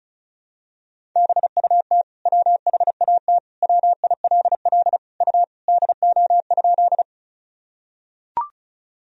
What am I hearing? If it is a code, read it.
Morse code, heard as a keyed tone, about 36 wpm, 700 Hz. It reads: BUT WHAT WILL U DO?
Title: Sentences from Top 100 Words and Abbreviations 35wpm